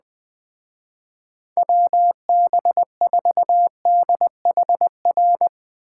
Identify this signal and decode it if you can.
Morse code, heard as a keyed tone, about 20 wpm, 700 Hz. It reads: WB4DHR
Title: Speed Racing - USA 2x3 Call Signs 17wpm